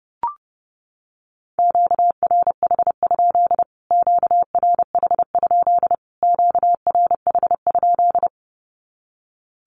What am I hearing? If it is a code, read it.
Morse code, heard as a keyed tone, about 30 wpm, 700 Hz. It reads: QRH? QRH? QRH?
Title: Q-Codes Repeated 3x 30wpm